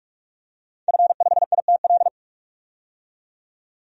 Morse code, heard as a keyed tone, about 42 wpm, 700 Hz. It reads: F5ITL
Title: DXCC-100 Call Signs - 5 Characters - 45wpm